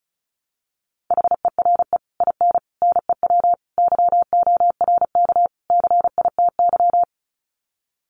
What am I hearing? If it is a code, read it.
Morse code, heard as a keyed tone, about 34 wpm, 700 Hz. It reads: HERE IN NEW YORK CITY
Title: Sets of 5 Words Encore2 35wpm